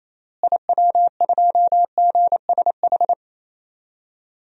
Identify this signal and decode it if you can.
Morse code, heard as a keyed tone, about 28 wpm, 700 Hz. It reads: IW2GSH
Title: DXCC-100 Call Signs - 6 Characters - Repeated 3x - 28wpm